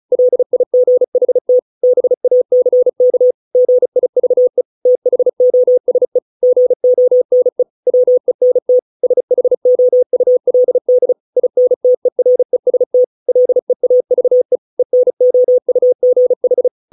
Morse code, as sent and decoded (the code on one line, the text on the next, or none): .-.. .. --. .... - / -... .- -.-. -.- / --. .. ...- . / - .... --- ... / --. --- -. / .-- . -. - / ... .... --- ..- .-.. -.. / .. -. - . .-. ... - / .-.. .- ...- . / . -. --- ..- --. ....
LIGHT BACK GIVE THOSE GONE WENT SHOULD INTEREST LEAVE ENOUGH